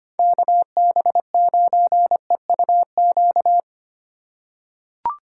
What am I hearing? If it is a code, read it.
Morse code, heard as a keyed tone, about 25 wpm, 700 Hz. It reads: KB9EUQ
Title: USA Call Signs - 6 Characters - 4x Character Spacing - 25wpm